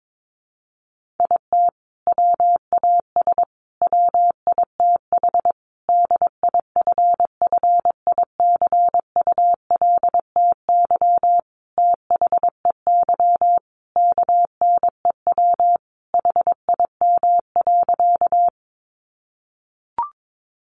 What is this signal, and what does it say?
Morse code, heard as a keyed tone, about 22 wpm, 700 Hz. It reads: IT WAS WITH DIFFICULTY THEY KNEW HIM.
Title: Sentences from Top 1500 Words 22wpm